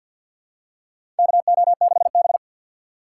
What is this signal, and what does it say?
Morse code, heard as a keyed tone, about 50 wpm, 700 Hz. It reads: KO6B
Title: USA Call Signs - 4 Characters - 6x Character Spacing - 50wpm